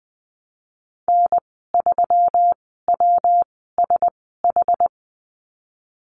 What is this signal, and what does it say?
Morse code, heard as a keyed tone, about 20 wpm, spaced as for 14 wpm, 700 Hz. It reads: N3WSH
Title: 140 Lesson 23g - Letter f - Call Signs 20wpm